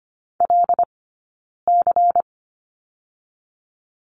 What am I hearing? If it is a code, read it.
Morse code, heard as a keyed tone, about 25 wpm, 700 Hz. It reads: L C
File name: course / 25:10wpm / 092 Lesson 16d - Letter c - Two Characters 25wpm